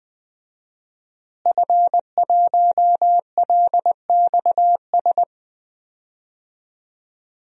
Morse code, heard as a keyed tone, about 20 wpm, 700 Hz. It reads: F1LXS
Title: DXCC-100 Call Signs - 5 Characters - Repeated 3x - 20wpm